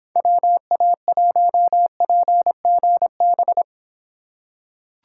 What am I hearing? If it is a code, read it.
Morse code, heard as a keyed tone, about 25 wpm, 700 Hz. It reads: WA1PGB